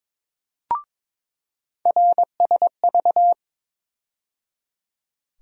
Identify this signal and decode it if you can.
Morse code, heard as a keyed tone, about 22 wpm, 700 Hz. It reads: RSV
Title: USA Call Sign Postfixes - 3 Letters 22wpm